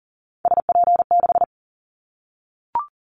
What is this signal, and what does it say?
Morse code, heard as a keyed tone, about 40 wpm, 700 Hz. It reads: SP6
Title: Callsign Prefixes for Top 25 DXCC - 40wpm